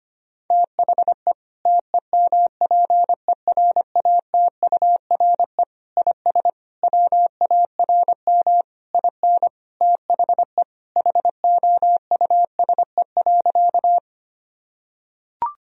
Morse code, heard as a keyed tone, about 25 wpm, 700 Hz. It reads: THE TEMPERATURE IS WARM IN THE HOUSE.
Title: Sentences from Top 1000 Words - Review Part 4 - 25wpm